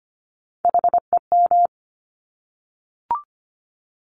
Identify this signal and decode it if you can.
Morse code, heard as a keyed tone, about 25 wpm, 700 Hz. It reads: HEM
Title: Top Three Letter Combinations in Words 25wpm